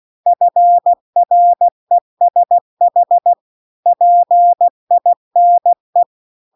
Morse code, heard as a keyed tone, about 16 wpm, 700 Hz. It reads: FRESH PINE
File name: CW-The-Big-Twoheaded-River-2-16WPM